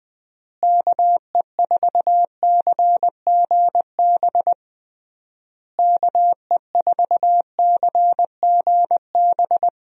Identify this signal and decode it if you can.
Morse code, heard as a keyed tone, about 21 wpm, 700 Hz. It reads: KE4CGB KE4CGB
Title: USA Call Signs - 6 Characters - Repeated 3x - 20wpm